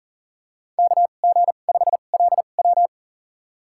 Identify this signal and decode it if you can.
Morse code, heard as a keyed tone, about 40 wpm, spaced as for 29 wpm, 700 Hz. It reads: KG5LW
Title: USA Call Signs - 5 Characters - 2x Character Spacing - 40wpm